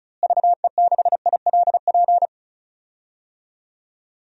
Morse code, heard as a keyed tone, about 36 wpm, 700 Hz. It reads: VE6ILP
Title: DXCC-100 Call Signs - 6 Characters - Repeated 2x - 35wpm